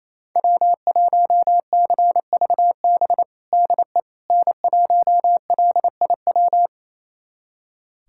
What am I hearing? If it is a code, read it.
Morse code, heard as a keyed tone, about 28 wpm, 700 Hz. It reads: W1CVB DE N1LIW
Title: USA Callsign DE Callsign - 5 Characters - 28wpm